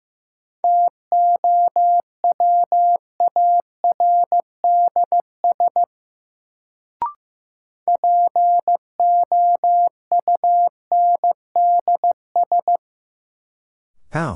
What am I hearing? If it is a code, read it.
Morse code, heard as a keyed tone, about 15 wpm, 700 Hz. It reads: TOWARDS POUNDS
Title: Sentences from Top 1000 Words - Review Part 2 - New Word Drill - 15wpm